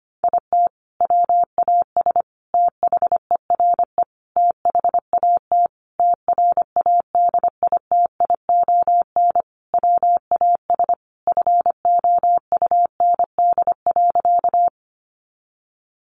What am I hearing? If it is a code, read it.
Morse code, heard as a keyed tone, about 25 wpm, 700 Hz. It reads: IT WAS THERE THAT TRADITION WAS FOUND.
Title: Sentences from Top 1500 Words 25wpm